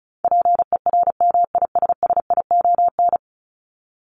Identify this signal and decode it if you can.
Morse code, heard as a keyed tone, about 34 wpm, 700 Hz. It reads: PERMISSION
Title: Top 1501-1750 Words 35wpm